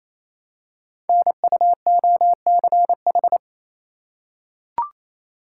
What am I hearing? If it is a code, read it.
Morse code, heard as a keyed tone, about 29 wpm, 700 Hz. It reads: NUOCH
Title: Code Groups - 5 Letters - 28wpm